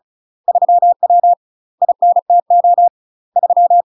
Morse code, heard as a keyed tone, about 36 wpm, 700 Hz. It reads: 3W INTO 3
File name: UK MASTER QSOs TRAINING / 35WPM / UK MASTER QSOs 0041